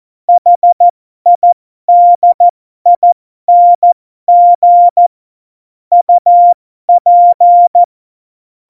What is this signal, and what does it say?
Morse code, heard as a keyed tone, about 14 wpm, spaced as for 12 wpm, 700 Hz. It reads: HIDING UP
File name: Morsekurs / Witze